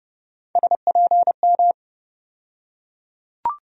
Morse code, heard as a keyed tone, about 31 wpm, 700 Hz. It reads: SPM
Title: USA Call Sign Postfixes - 3 Letters 30wpm